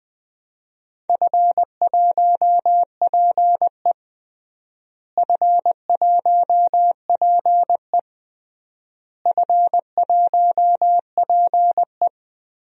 Morse code, heard as a keyed tone, about 20 wpm, 700 Hz. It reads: F1PE F1PE F1PE
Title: DXCC-100 Call Signs - 4 Characters - Repeated 3x - 20wpm